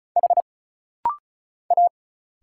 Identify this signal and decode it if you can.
Morse code, heard as a keyed tone, about 34 wpm, 700 Hz. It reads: H A
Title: Single Letters - Rapid-Fire 35wpm